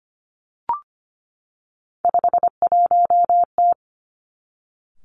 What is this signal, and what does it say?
Morse code, heard as a keyed tone, about 25 wpm, 700 Hz. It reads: 51T
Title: Three Challenging Letter-Number Combinations 25wpm